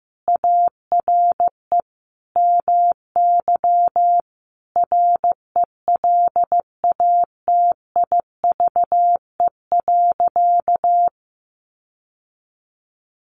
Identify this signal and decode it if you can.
Morse code, heard as a keyed tone, about 15 wpm, 700 Hz. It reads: ARE MY RELATIVE.
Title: Sentences from Top 1400 Words 15wpm